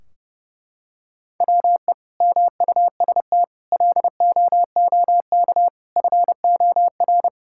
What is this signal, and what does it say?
Morse code, heard as a keyed tone, about 30 wpm, 700 Hz. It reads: WE MUST LOOK FOR
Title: Sentences from Top 1600 Words 30wpm